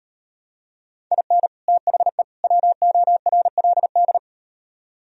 Morse code, heard as a keyed tone, about 38 wpm, 700 Hz. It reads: IN THE WORLD